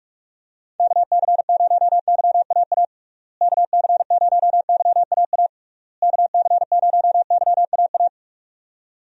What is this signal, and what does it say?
Morse code, heard as a keyed tone, about 46 wpm, 700 Hz. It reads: KC0YAA KC0YAA KC0YAA
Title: USA Call Signs - 6 Characters - Repeated 3x - 45wpm